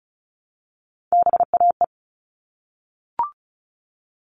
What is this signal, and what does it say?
Morse code, heard as a keyed tone, about 34 wpm, 700 Hz. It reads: BAE
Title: All Cut Numbers - 3-Digit 35wpm